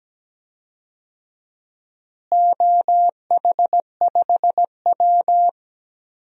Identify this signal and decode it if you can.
Morse code, heard as a keyed tone, about 17 wpm, 700 Hz. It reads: OH5W
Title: DXCC-100 Call Signs - 4 Characters - Repeated 3x - 17wpm